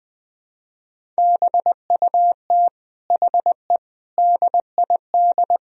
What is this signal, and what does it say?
Morse code, heard as a keyed tone, about 21 wpm, 700 Hz. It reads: BUT HE DID
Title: Sentences from Top 1000 Words - Review Part 1 - 20wpm